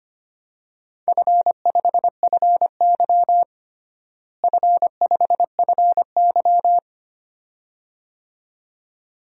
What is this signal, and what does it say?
Morse code, heard as a keyed tone, about 25 wpm, 700 Hz. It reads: F5FY F5FY
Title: DXCC-100 Call Signs - 4 Characters - Repeated 2x - 25wpm